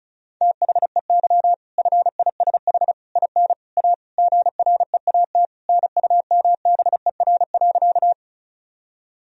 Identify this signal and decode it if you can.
Morse code, heard as a keyed tone, about 34 wpm, 700 Hz. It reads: THEY FISH IN A GREAT NUMBER.
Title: Sentences from Top 1000 Words - Review Part 1 - 35wpm